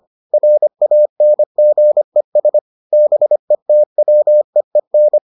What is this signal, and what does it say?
Morse code, heard as a keyed tone, about 25 wpm, 600 Hz. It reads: RANGES BETWEEN